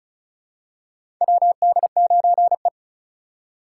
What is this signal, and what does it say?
Morse code, heard as a keyed tone, about 34 wpm, 700 Hz. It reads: WD9E
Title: USA Call Signs - 4 Characters - 2x Character Spacing - 35wpm